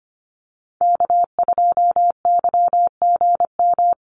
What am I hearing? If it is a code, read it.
Morse code, heard as a keyed tone, about 25 wpm, 700 Hz. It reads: K2YGM
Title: Call Signs Repeated 3x - 25wpm